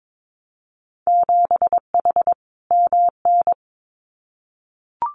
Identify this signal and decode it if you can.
Morse code, heard as a keyed tone, about 22 wpm, 700 Hz. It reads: 7H MN
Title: Winter Field Day Exchanges Encore 22wpm